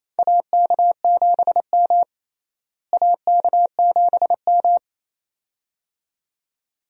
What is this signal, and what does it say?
Morse code, heard as a keyed tone, about 28 wpm, 700 Hz. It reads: AK7M AK7M
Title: USA Call Signs - 4 Characters - Repeated 2x - 28wpm